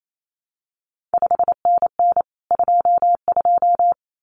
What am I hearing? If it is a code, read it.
Morse code, heard as a keyed tone, about 28 wpm, 700 Hz. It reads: 5NN 22
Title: CQ WW DX Contest Exchange Encore - 28wpm